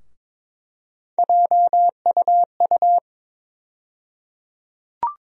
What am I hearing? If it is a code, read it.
Morse code, heard as a keyed tone, about 22 wpm, 700 Hz. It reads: JUU
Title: USA Call Sign Postfixes - 3 Letters 22wpm